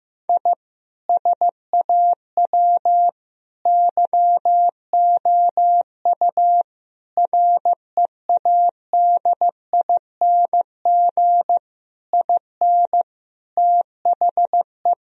Morse code, heard as a keyed tone, about 15 wpm, 700 Hz. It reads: I SAW YOU READING IN THE
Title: Sentences from Top 1000 Words - Review Part 4 - 15wpm